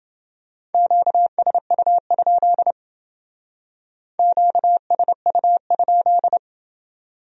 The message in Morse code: QSU? QSU?

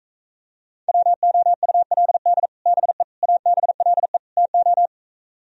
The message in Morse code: WOULD BE ABLE TO

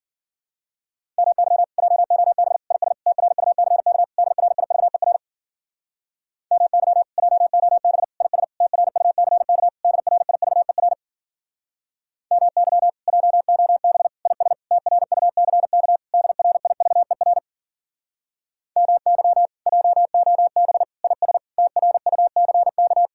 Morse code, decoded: MY JOB IS TRUCK DRIVER MY JOB IS TRUCK DRIVER MY JOB IS TRUCK DRIVER MY JOB IS TRUCK

